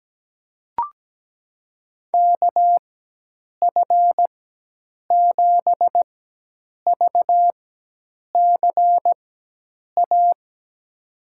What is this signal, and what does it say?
Morse code, heard as a keyed tone, about 17 wpm, spaced as for 8 wpm, 700 Hz. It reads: KF7VCA
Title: USA Call Signs - 6 Characters - 4x Character Spacing - 17wpm